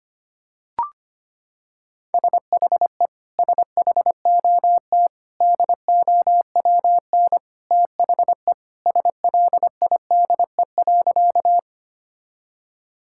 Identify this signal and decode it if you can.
Morse code, heard as a keyed tone, about 25 wpm, 700 Hz. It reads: SHE SHOT DOWN THE SLIDE.
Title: Sentences from Top 1600 Words 25wpm